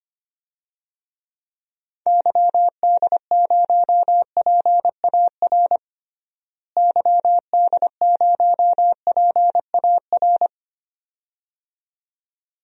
Morse code, heard as a keyed tone, about 25 wpm, 700 Hz. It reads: YD0PAR YD0PAR
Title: DXCC-100 Call Signs - 6 Characters - Repeated 2x - 25wpm